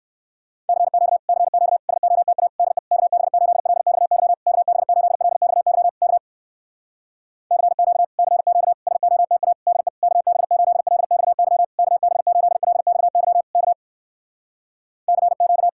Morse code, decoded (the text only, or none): CQ CQ SOTA DE KB8LXY KB8LXY K CQ CQ SOTA DE KB8LXY KB8LXY K CQ